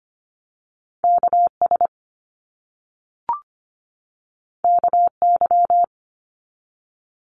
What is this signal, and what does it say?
Morse code, heard as a keyed tone, about 25 wpm, 700 Hz. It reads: KS KY
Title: US States 25wpm